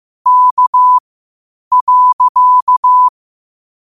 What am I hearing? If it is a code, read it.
Morse code, heard as a keyed tone, about 15 wpm, spaced as for 8 wpm, 1000 Hz. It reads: K.
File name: qotdfiles / 8wpm